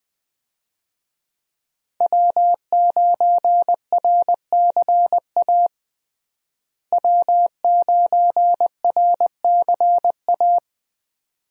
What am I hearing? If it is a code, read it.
Morse code, heard as a keyed tone, about 20 wpm, 700 Hz. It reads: W9RCA W9RCA
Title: USA Call Signs - 5 Characters - Repeated 2x - 20wpm